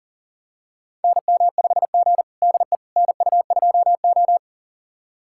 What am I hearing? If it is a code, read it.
Morse code, heard as a keyed tone, about 38 wpm, 700 Hz. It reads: NM5G DE NU2O
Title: USA Callsign DE Callsign - 4 Characters - 40wpm